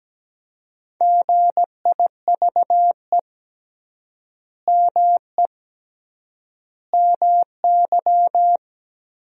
Morse code, heard as a keyed tone, about 17 wpm, 700 Hz. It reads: GIVE ME MY
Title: 3x Word-Spacing - Sentences from Top 200 Words - 17wpm